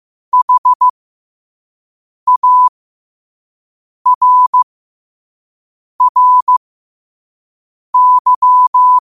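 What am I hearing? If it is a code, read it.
Morse code, heard as a keyed tone, about 15 wpm, spaced as for 5 wpm, 1000 Hz. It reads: HARRY
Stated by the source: Morse code sent at 5 wpm for learning.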